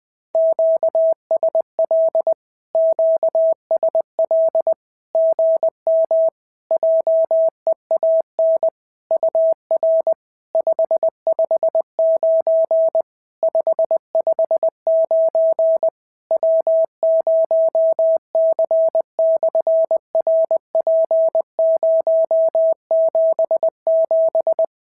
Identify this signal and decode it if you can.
Morse code, heard as a keyed tone, about 20 wpm, 650 Hz. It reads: QSL QSL GM JEAN UR 559 559 W0C/RP077
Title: SOTA QSO Practice 20wpm